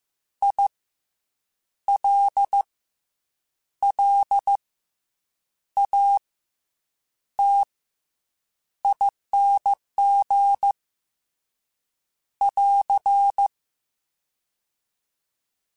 Morse code, heard as a keyed tone, about 15 wpm, 800 Hz. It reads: I L L A T ING <AR>
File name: Special INGs / Special INGs-mp3s / SlowToFaster ing @13